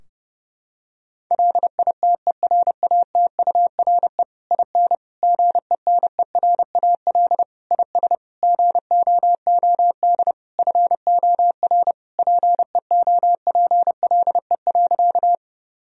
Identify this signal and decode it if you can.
Morse code, heard as a keyed tone, about 30 wpm, 700 Hz. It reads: LITERATURE IN GENERAL IS GOOD FOR PEOPLE.
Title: Sentences from Top 1000 Words - Review Part 3 - 30wpm